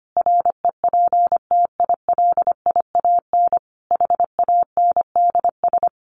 Morse code, heard as a keyed tone, about 25 wpm, 700 Hz. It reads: REPTILIAN HANDS
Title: A Sound Of Thunder 25wpm